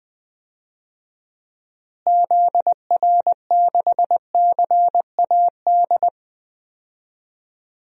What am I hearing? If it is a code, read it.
Morse code, heard as a keyed tone, about 21 wpm, 700 Hz. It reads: ZR6CAD
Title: DXCC-100 Call Signs - 6 Characters - Repeated 2x - 20wpm